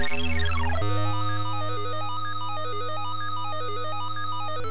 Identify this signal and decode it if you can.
There is no code.